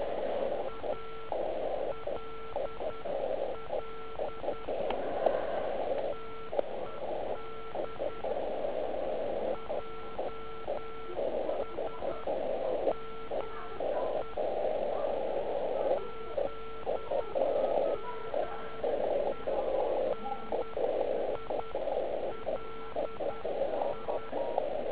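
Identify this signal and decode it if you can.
Morse code, heard as a keyed tone, about 10 wpm, spaced as for 8 wpm, 1340 Hz. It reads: ALL TED JSME ZMENILI
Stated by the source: Slow (QRS) CW test by beginner OK1IFW during beginner Morse skeds on 3559.5 kHz (80 m).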